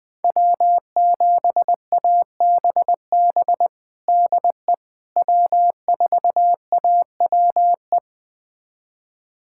Morse code, heard as a keyed tone, about 20 wpm, 700 Hz. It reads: W7ABB DE W4AWE
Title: USA Callsign DE Callsign - 5 Characters - 20wpm